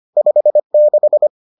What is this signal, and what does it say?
Morse code, heard as a keyed tone, about 25 wpm, 600 Hz. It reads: 56